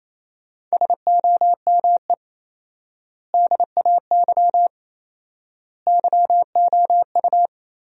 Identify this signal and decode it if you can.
Morse code, heard as a keyed tone, about 28 wpm, 700 Hz. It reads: SOME DAY YOU
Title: 4x Word-Spacing - Sentences from Top 200 Words - 28wpm